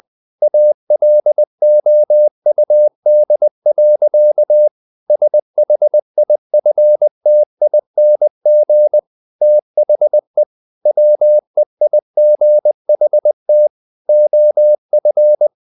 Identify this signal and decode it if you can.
Morse code, heard as a keyed tone, about 20 wpm, 600 Hz. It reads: ALOUD. SHIFTING THE WEIGHT OF
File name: the old man and the sea / the old man and the sea part 1 20wpm